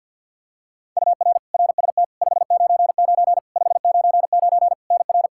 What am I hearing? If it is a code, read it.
Morse code, heard as a keyed tone, about 52 wpm, 700 Hz. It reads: UR RST 599 599 NR
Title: Practice QSO - Exchange 1 - Beginning - 50wpm